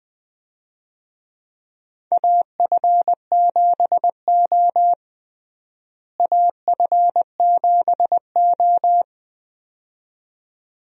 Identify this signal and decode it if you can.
Morse code, heard as a keyed tone, about 20 wpm, 700 Hz. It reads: AF7O AF7O
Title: USA Call Signs - 4 Characters - Repeated 2x - 20wpm